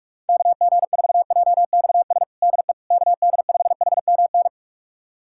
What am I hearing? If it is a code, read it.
Morse code, heard as a keyed tone, about 46 wpm, 700 Hz. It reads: KG4JXS DE KD5HMN
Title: USA Callsign DE Callsign - 6 Characters - 45wpm